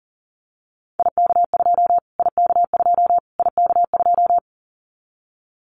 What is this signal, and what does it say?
Morse code, heard as a keyed tone, about 40 wpm, 700 Hz. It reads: IK2 IK2 IK2
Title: Callsign Prefixes for Top 25 DXCC - Repeated 3x - 40wpm